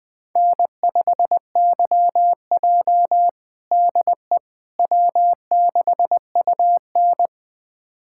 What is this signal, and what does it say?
Morse code, heard as a keyed tone, about 21 wpm, 700 Hz. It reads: N5YJ DE W6UN
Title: USA Callsign DE Callsign - 4 Characters - 20wpm